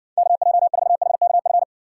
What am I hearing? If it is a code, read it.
Morse code, heard as a keyed tone, about 60 wpm, 700 Hz. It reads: KJ4HKF